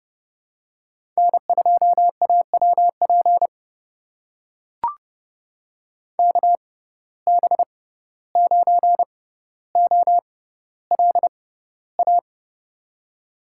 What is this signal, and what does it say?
Morse code, heard as a keyed tone, about 30 wpm, spaced as for 15 wpm, 700 Hz. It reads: N2AWP KB9OLA
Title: USA Call Signs - 6x Character Spacing - 30wpm